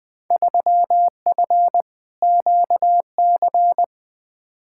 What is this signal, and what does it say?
Morse code, heard as a keyed tone, about 20 wpm, 700 Hz. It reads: 3F QC
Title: Field Day Exchanges - 20wpm